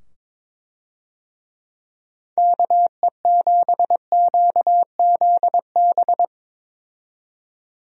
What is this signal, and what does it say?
Morse code, heard as a keyed tone, about 22 wpm, 700 Hz. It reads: KE7QZB
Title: USA Call Signs - 6 Characters - 22wpm